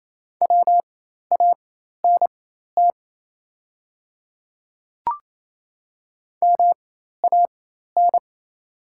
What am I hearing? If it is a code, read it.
Morse code, heard as a keyed tone, about 28 wpm, spaced as for 13 wpm, 700 Hz. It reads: WANT MAN
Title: 4x Character Spacing - Top 200 Words - 28wpm